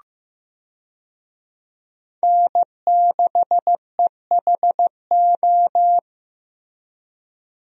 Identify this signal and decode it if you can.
Morse code, heard as a keyed tone, about 15 wpm, 700 Hz. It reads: N6EHO
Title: USA Call Signs - 5 Characters - Repeated 2x - 15wpm